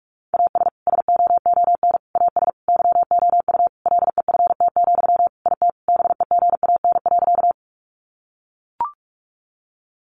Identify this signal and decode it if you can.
Morse code, heard as a keyed tone, about 44 wpm, 700 Hz. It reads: AS SOON AS YOU LEFT, IT BEGAN.
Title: Sentences from Top 400 Words 45wpm